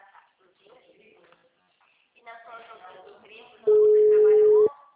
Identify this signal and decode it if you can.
Ringback tone (Europe, 425 Hz), with speech.
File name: out- -2012- - - .7473